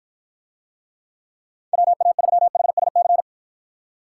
Morse code, heard as a keyed tone, about 52 wpm, 700 Hz. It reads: WA3HSC